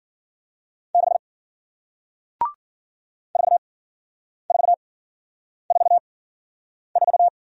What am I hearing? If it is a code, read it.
Morse code, heard as a keyed tone, about 52 wpm, spaced as for 9 wpm, 700 Hz. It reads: 6 4444